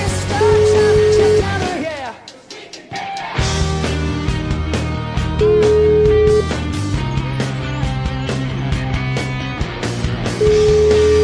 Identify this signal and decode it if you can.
Ringback tone (Europe, 425 Hz), with noise.